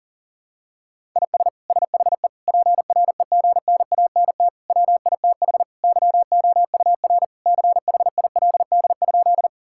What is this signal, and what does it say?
Morse code, heard as a keyed tone, about 40 wpm, 700 Hz. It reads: IS SHE PREGNANT WITH YOUR CHILD?
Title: Sentences from Top 1000 Words - Review Part 6 - 40wpm